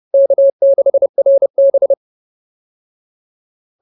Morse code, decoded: K6RB

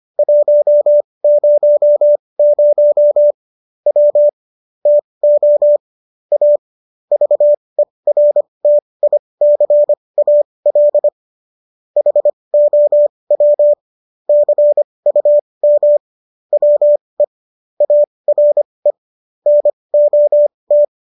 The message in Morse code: 100 W TO A VERTICAL HOW CUM WE ARE NOT

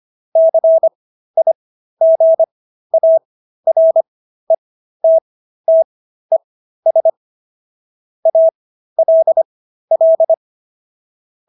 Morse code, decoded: CIGARETTES ALL

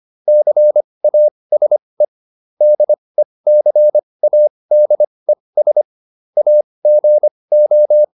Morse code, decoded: CASE DECADES AGO

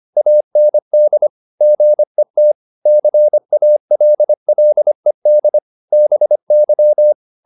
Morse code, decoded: AND GET CALLED BY

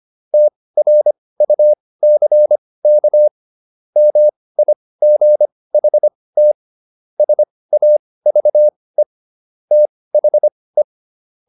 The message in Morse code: TRUCK MIGHT SAVE THE

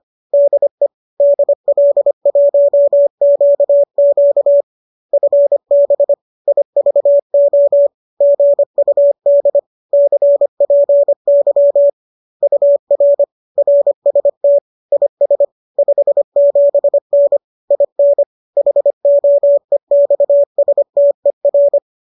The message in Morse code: DE DL1QQ FB IVO GUD CPY UR RST IS 57N IN HOEXSTER

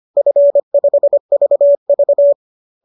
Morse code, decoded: F5VV

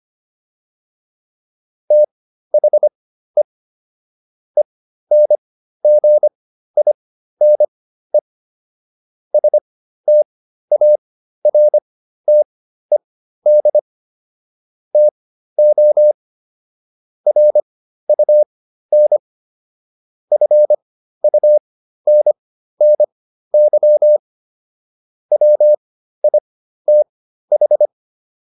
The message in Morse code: THE ENGINE STARTED TO RUN FUNNY WITH

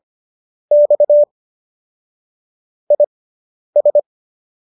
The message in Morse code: X IS

